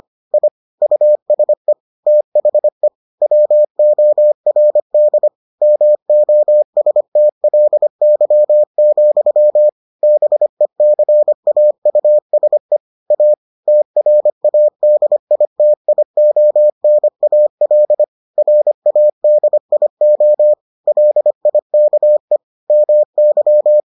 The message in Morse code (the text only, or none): I USE THE WORD MOSTLY, BECAUSE A TRADITIONAL RADIO LIKE MY